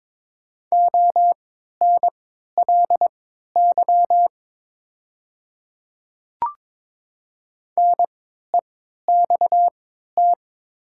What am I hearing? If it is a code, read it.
Morse code, heard as a keyed tone, about 22 wpm, spaced as for 12 wpm, 700 Hz. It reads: ONLY NEXT
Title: 3x Character Spacing - Top 300 Words - 22wpm